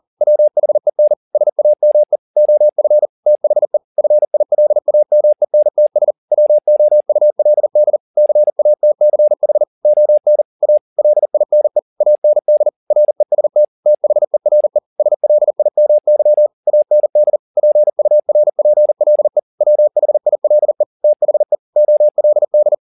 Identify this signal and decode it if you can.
Morse code, heard as a keyed tone, about 40 wpm, 600 Hz. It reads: WHEN SAME OF THE FILAMENTS WOULD CATCH ON A LINE AND REST THERE SLIMY AND PURPLE WHILE THE OLD